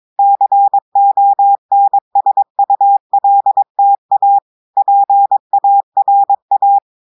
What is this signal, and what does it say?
Morse code, heard as a keyed tone, about 22 wpm, 800 Hz. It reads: CONSULTA PARA
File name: audios / portugais / portugais texte en clair 1 vitesse 22 mots minutes